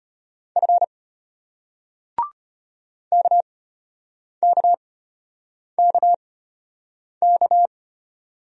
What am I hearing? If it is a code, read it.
Morse code, heard as a keyed tone, about 34 wpm, spaced as for 8 wpm, 700 Hz. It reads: F KKKK